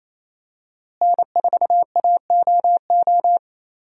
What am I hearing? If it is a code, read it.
Morse code, heard as a keyed tone, about 28 wpm, 700 Hz. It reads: N4AOO
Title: USA Call Signs - 5 Characters - Repeated 2x - 28wpm